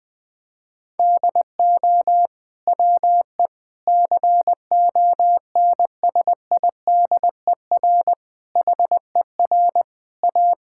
Morse code, heard as a keyed tone, about 20 wpm, 700 Hz. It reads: DO WE CONSIDER HER A
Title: Sentences from Top 1000 Words - Review Part 5 - 20wpm